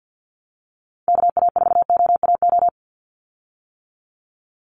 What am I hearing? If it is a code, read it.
Morse code, heard as a keyed tone, about 50 wpm, 700 Hz. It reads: KA4OAO
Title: USA Call Signs 50wpm